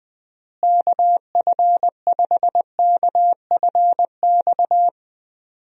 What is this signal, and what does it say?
Morse code, heard as a keyed tone, about 20 wpm, 700 Hz. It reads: KF5KFX
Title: USA Call Signs - 6 Characters - Repeated 3x - 20wpm